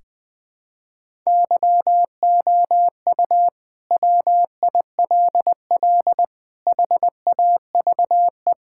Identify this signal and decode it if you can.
Morse code, heard as a keyed tone, about 21 wpm, 700 Hz. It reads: YOU WILL HAVE
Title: Sentences from Top 1000 Words - Review Part 5 - 20wpm